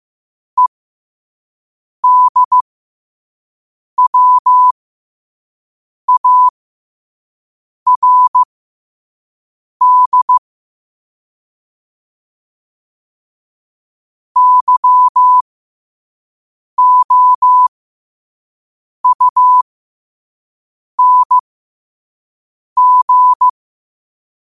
Morse code, heard as a keyed tone, about 15 wpm, spaced as for 5 wpm, 1000 Hz. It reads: EDWARD YOUNG
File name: qotdfiles / 5wpm